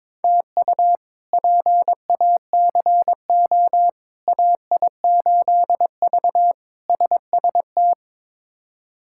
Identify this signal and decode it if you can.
Morse code, heard as a keyed tone, about 22 wpm, 700 Hz. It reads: TU PACO AI8V SST
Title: K1USN SST QSO Practice - 22wpm